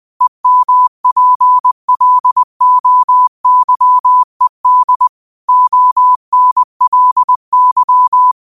Morse code, heard as a keyed tone, about 20 wpm, 1000 Hz. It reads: EMPLOYED ONLY
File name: qotdfiles / 20wpm